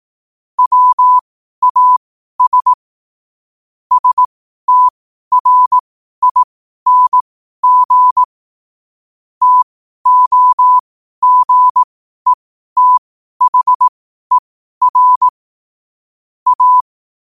Morse code, heard as a keyed tone, about 18 wpm, spaced as for 12 wpm, 1000 Hz. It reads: WAS STRING TOGETHER A
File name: qotdfiles / 12wpm